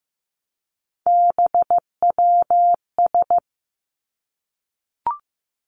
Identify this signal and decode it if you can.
Morse code, heard as a keyed tone, about 15 wpm, 700 Hz. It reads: BWS
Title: Three Challenging Letter-Number Combinations 15wpm